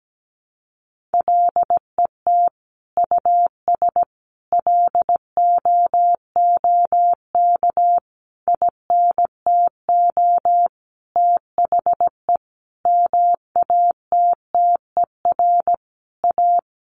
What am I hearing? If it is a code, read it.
Morse code, heard as a keyed tone, about 17 wpm, 700 Hz. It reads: LET US LOOK INTO THE MATTER A
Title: Sentences from Top 1400 Words 17wpm